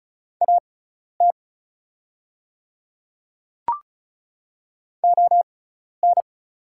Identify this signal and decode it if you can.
Morse code, heard as a keyed tone, about 36 wpm, spaced as for 12 wpm, 700 Hz. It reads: AT ON
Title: 6x Character Spacing - Top 100 Words - 35wpm